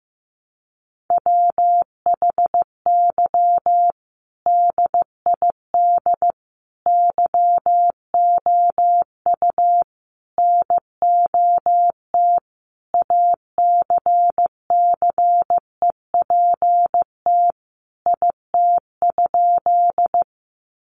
Morse code, heard as a keyed tone, about 15 wpm, 700 Hz. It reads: WHY DID YOU NOT ACCEPT IT?
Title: Sentences from Top 1300 Words 15wpm